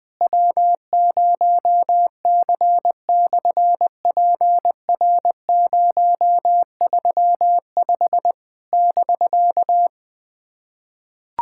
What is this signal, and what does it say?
Morse code, heard as a keyed tone, about 20 wpm, 700 Hz. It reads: W0C/PR035 <BK>
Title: SOTA QSO Practice 20wpm